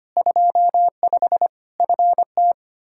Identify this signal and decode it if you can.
Morse code, heard as a keyed tone, about 25 wpm, 700 Hz. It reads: 25 FT
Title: Practice QSO - Exchange 2 - Info - 25wpm